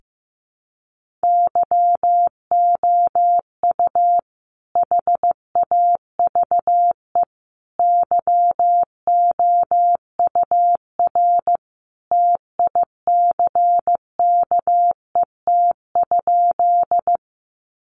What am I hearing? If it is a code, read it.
Morse code, heard as a keyed tone, about 15 wpm, 700 Hz. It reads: YOU HAVE YOUR TICKET?
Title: Sentences from Top 1400 Words 15wpm